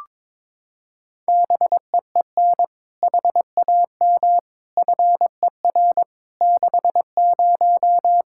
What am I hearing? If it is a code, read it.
Morse code, heard as a keyed tone, about 22 wpm, 700 Hz. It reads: BEEN HAM FER 60
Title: Practice QSO - Exchange 3 - More Info - 22wpm